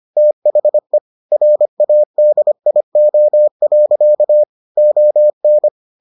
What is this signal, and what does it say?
Morse code, heard as a keyed tone, about 25 wpm, 600 Hz. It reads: THE RADIO. ON